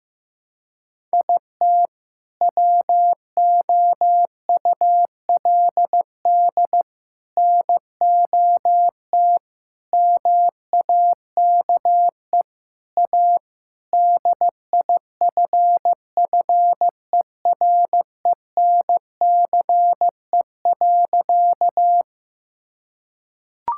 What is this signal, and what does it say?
Morse code, heard as a keyed tone, about 15 wpm, 700 Hz. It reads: IT WOULD NOT MAKE A DIFFERENCE.
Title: Sentences from Top 1000 Words - Review Part 5 - 15wpm